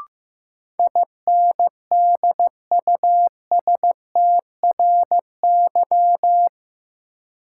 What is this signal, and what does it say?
Morse code, heard as a keyed tone, about 15 wpm, 700 Hz. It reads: INDUSTRY
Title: Sentences from Top 1000 Words - Review Part 4 - New Word Drill - 15wpm